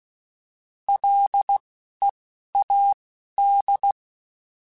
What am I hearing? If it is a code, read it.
Morse code, heard as a keyed tone, about 16 wpm, 800 Hz. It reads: L E A D